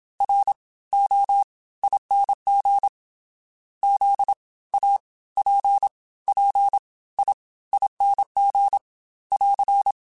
Morse code, heard as a keyed tone, about 27 wpm, spaced as for 18 wpm, 800 Hz. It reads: ROING ZAPPIING <AR>